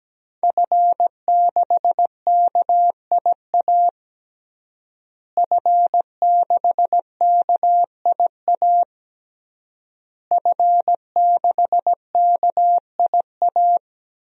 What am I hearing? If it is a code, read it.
Morse code, heard as a keyed tone, about 17 wpm, 700 Hz. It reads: F6KIA F6KIA F6KIA
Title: DXCC-100 Call Signs - 5 Characters - Repeated 3x - 17wpm